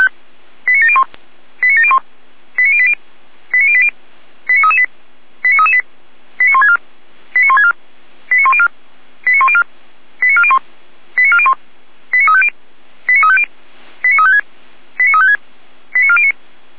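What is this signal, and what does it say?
Five-tone selective call (ZVEI), heard as quick runs of five tones, tones of about 70 ms. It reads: 89701 89701 89080 89080 89338 89338 89164 89164 89104 89104 89401 89401 89350 89350 89356 89356 89409
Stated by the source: ZVEI five-tone selective call received via SDR.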